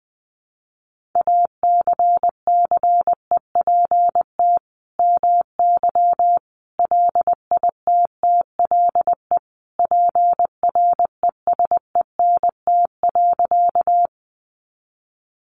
Morse code, heard as a keyed tone, about 20 wpm, 700 Hz. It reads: ACCEPT MY LITTLE PRESENT.
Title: Sentences from Top 1300 Words 20wpm